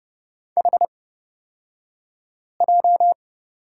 Morse code, heard as a keyed tone, about 30 wpm, spaced as for 5 wpm, 700 Hz. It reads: HJ
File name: course / 30:10wpm / 235 Lesson 38d - Letter q - Two Characters 30wpm